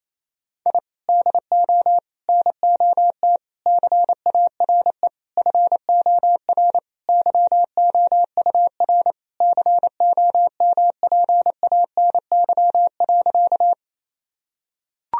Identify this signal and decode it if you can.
Morse code, heard as a keyed tone, about 28 wpm, 700 Hz. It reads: I DO NOT CARE FOR YOUR COMPANY.
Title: Sentences from Top 1000 Words - Review Part 3 - 28wpm